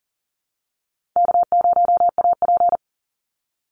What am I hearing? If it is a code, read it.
Morse code, heard as a keyed tone, about 40 wpm, 700 Hz. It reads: K0AP
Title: Easy Call Signs Repeated 2x - 40wpm